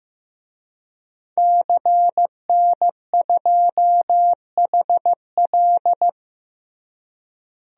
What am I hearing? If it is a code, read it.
Morse code, heard as a keyed tone, about 15 wpm, 700 Hz. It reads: CN2HL
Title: DXCC-100 Call Signs - 5 Characters - Repeated 3x - 15wpm